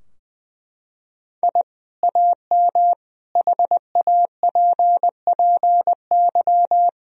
Morse code, heard as a keyed tone, about 20 wpm, 700 Hz. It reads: I AM HAPPY
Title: Sentences from Top 1000 Words - Review Part 3 - 20wpm